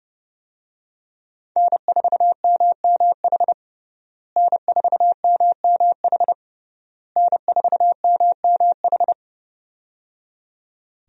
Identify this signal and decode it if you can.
Morse code, heard as a keyed tone, about 30 wpm, 700 Hz. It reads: N4MMH N4MMH N4MMH
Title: USA Call Signs - 5 Characters - Repeated 3x - 30wpm